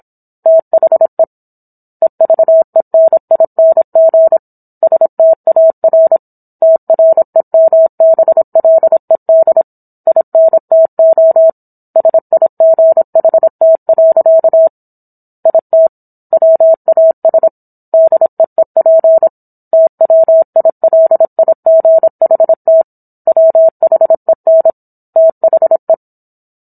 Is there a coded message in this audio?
Yes: Morse code from 0.5 to 25.9 s, heard as a keyed tone, about 27 wpm, 650 Hz. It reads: THE EVENING STAR TREMBLED INTO SIGHT. IT WAS DEEP TWILIGHT WHEN THE